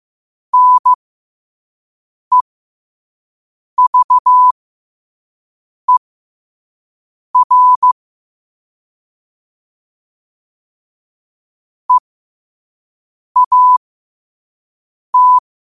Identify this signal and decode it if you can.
Morse code, heard as a keyed tone, about 15 wpm, spaced as for 5 wpm, 1000 Hz. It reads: NEVER EAT
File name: qotdfiles / 5wpm